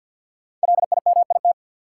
Morse code, heard as a keyed tone, about 50 wpm, 700 Hz. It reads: LIMIT
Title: Sentences from Top 1500 Words - New Word Drill Alt - 50wpm